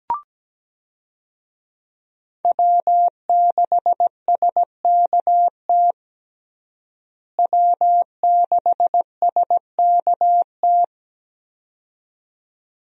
Morse code, heard as a keyed tone, about 17 wpm, 700 Hz. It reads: W6SKT W6SKT
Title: USA Call Signs - 5 Characters - Repeated 2x - 17wpm